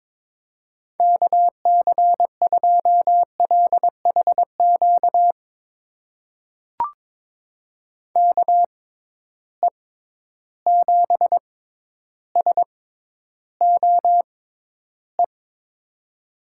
Morse code, heard as a keyed tone, about 21 wpm, spaced as for 11 wpm, 700 Hz. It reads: KC2LHQ KE7SOE
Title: USA Call Signs - 6 Characters - 6x Character Spacing - 22wpm